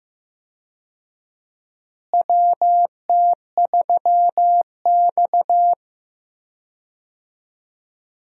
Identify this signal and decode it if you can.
Morse code, heard as a keyed tone, about 16 wpm, 700 Hz. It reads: WT3X
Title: USA Call Signs - 4 Characters - Repeated 2x - 15wpm